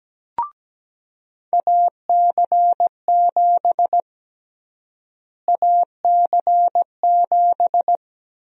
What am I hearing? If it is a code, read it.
Morse code, heard as a keyed tone, about 17 wpm, 700 Hz. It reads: AC7 AC7
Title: USA Call Sign Prefixes - Repeated 2x - 17wpm